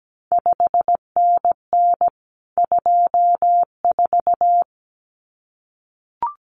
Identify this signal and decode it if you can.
Morse code, heard as a keyed tone, about 17 wpm, 700 Hz. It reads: 5NN 24
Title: CQ WW DX Contest Exchange - 17wpm